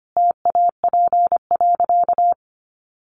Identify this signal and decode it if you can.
Morse code, heard as a keyed tone, about 25 wpm, 700 Hz. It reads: TAP.
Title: Sentences from Top 1400 Words 25wpm